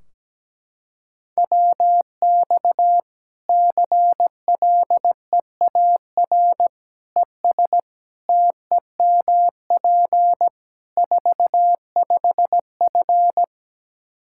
Morse code, heard as a keyed tone, about 17 wpm, 700 Hz. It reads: WX CLEAR ES TEMP 45F
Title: Practice QSO - Exchange 2 - Info - 17wpm